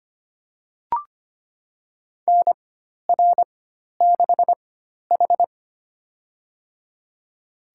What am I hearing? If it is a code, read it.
Morse code, heard as a keyed tone, about 25 wpm, spaced as for 12 wpm, 700 Hz. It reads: NR6H
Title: USA Call Signs - 4x Character Spacing - 25wpm